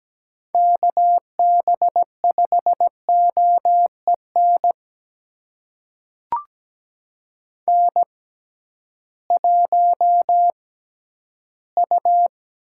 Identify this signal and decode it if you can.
Morse code, heard as a keyed tone, about 17 wpm, spaced as for 11 wpm, 700 Hz. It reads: KB5OEN N1U
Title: USA Call Signs - 6x Character Spacing - 17wpm